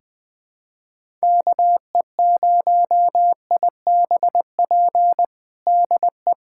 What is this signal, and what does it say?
Morse code, heard as a keyed tone, about 20 wpm, 700 Hz. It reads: KE0IBP DE
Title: USA Callsign DE Callsign - 6 Characters - 20wpm